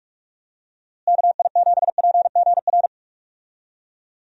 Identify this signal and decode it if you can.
Morse code, heard as a keyed tone, about 46 wpm, 700 Hz. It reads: KI7PGR
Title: USA Call Signs - 6 Characters - Repeated 3x - 45wpm